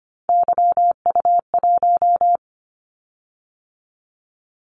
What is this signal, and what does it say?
Morse code, heard as a keyed tone, about 25 wpm, 700 Hz. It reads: YU1
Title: Callsign Prefixes for Top 25 DXCC - 25wpm